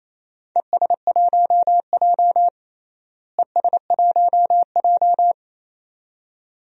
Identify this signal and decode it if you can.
Morse code, heard as a keyed tone, about 28 wpm, 700 Hz. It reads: ES1J ES1J
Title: DXCC-100 Call Signs - 4 Characters - Repeated 2x - 28wpm